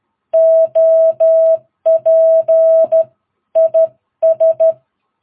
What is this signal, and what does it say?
Morse code, heard as a keyed tone, about 12 wpm, 650 Hz. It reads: OP IS
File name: jh8cbh / 2022cw